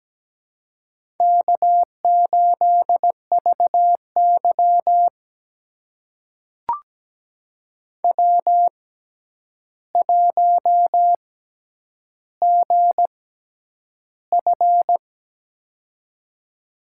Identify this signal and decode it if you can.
Morse code, heard as a keyed tone, about 17 wpm, spaced as for 9 wpm, 700 Hz. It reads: K8VY W1GF
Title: USA Call Signs - 4 Characters - 6x Character Spacing - 17wpm